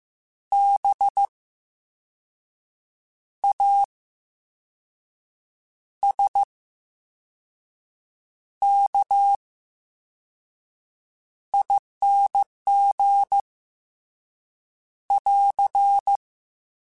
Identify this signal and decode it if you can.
Morse code, heard as a keyed tone, about 15 wpm, 800 Hz. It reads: B A S K I N G <AR>
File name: Special INGs-mp3s / Special INGs-mp3s / VerySlowToFaster Ing @13